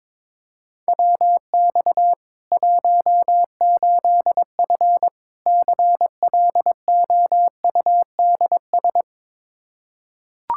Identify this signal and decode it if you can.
Morse code, heard as a keyed tone, about 22 wpm, 700 Hz. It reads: WX 18F CLOUDS
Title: Practice QSO - Exchange 2 - Info - 22wpm